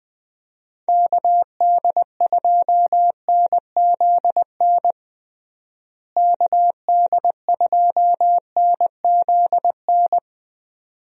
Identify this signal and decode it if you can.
Morse code, heard as a keyed tone, about 21 wpm, 700 Hz. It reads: KD2NZN KD2NZN
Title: USA Call Signs - 6 Characters - Repeated 2x - 20wpm